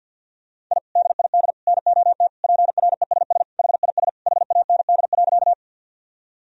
Morse code, heard as a keyed tone, about 50 wpm, 700 Hz. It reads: I DID NOT PRESS HIS HAND.
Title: Sentences from Top 1000 Words - Review Part 1 - 50wpm